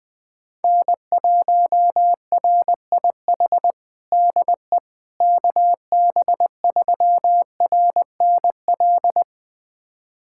Morse code, heard as a keyed tone, about 20 wpm, 700 Hz. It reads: N1RIH DE KB3RNL
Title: USA Callsign DE Callsign 20wpm